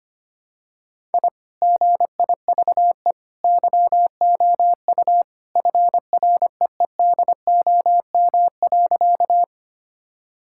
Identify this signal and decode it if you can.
Morse code, heard as a keyed tone, about 25 wpm, 700 Hz. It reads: I GIVE YOU FREEDOM.
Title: Sentences from Top 1000 Words - Review Part 5 - 25wpm